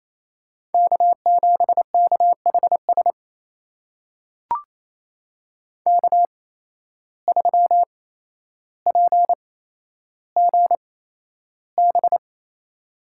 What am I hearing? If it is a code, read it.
Morse code, heard as a keyed tone, about 28 wpm, spaced as for 12 wpm, 700 Hz. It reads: K7KHS K3PGB
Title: USA Call Signs - 5 Characters - 8x Character Spacing - 28wpm